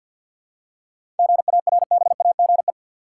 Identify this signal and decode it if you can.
Morse code, heard as a keyed tone, about 50 wpm, 700 Hz. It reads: GARBAGE
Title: Seven Letter Words - 50wpm